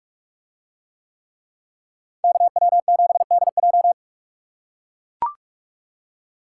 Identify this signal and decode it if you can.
Morse code, heard as a keyed tone, about 44 wpm, 700 Hz. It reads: KW7DJ